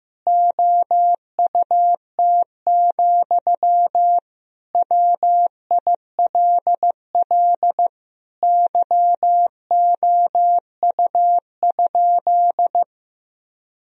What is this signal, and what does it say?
Morse code, heard as a keyed tone, about 15 wpm, 700 Hz. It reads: OUT, WILL YOU?
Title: Sentences from Top 1000 Words - Review Part 1 - 15wpm